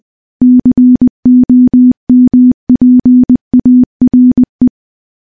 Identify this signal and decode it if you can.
Morse code, heard as a keyed tone, about 20 wpm, 250 Hz. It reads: COMPARE
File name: morse-training-20WPM